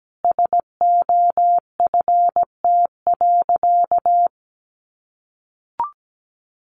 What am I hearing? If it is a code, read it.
Morse code, heard as a keyed tone, about 17 wpm, 700 Hz. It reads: SOFT.
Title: Easy Phrases 17wpm